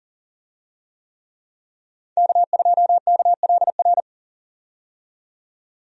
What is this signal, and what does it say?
Morse code, heard as a keyed tone, about 40 wpm, 700 Hz. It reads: K2KLR
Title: USA Call Signs - 5 Characters - 40wpm